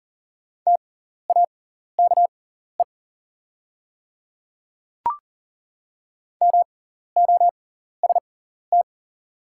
Morse code, heard as a keyed tone, about 40 wpm, spaced as for 14 wpm, 690 Hz. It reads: TAKE MOST
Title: 6x Character Spacing - Top 300 Words - 40wpm